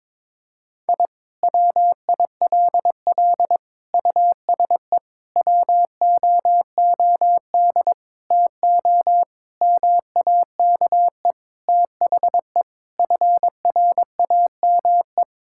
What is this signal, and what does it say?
Morse code, heard as a keyed tone, about 22 wpm, 700 Hz. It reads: I WILL USE WOOD TO MAKE THE FRAME
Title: Sentences from Top 1000 Words - Review Part 6 - 22wpm